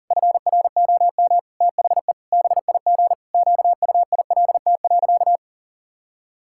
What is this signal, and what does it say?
Morse code, heard as a keyed tone, about 40 wpm, 700 Hz. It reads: FROM THE BIG QUILT.
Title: Pangram Challenge v2 - 40wpm